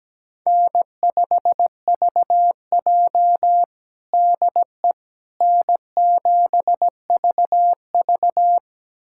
Morse code, heard as a keyed tone, about 17 wpm, 700 Hz. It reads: N5VJ DE N7VV
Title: USA Callsign DE Callsign - 4 Characters - 17wpm